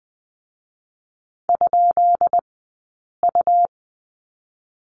Morse code, heard as a keyed tone, about 20 wpm, 700 Hz. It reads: ? U
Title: 204 Lesson 33d - Letter j - Two Characters 20wpm